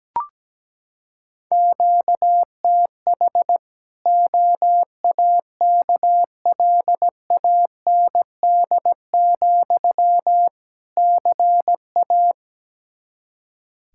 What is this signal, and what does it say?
Morse code, heard as a keyed tone, about 17 wpm, 700 Hz. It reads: QTH OAKLAND, CA
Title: K1USN SST QSO Practice 17wpm